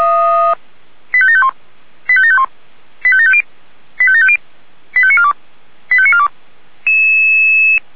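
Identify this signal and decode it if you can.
Five-tone selective call (ZVEI), heard as quick runs of five tones, tones of about 70 ms. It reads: 85751 85751 85750 85750 85942 85942
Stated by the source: Received on an SDR; ZVEI five-tone call 85730.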